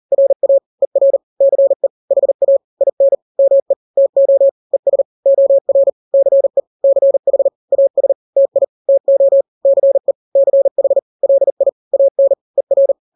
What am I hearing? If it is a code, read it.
Morse code, heard as a keyed tone, about 40 wpm, 550 Hz. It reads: RA ER CE HA IN ME TO ES OR CE CH AS TI TO CE CH LI AN ER
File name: n-grams / 2-grams 40 35 WPM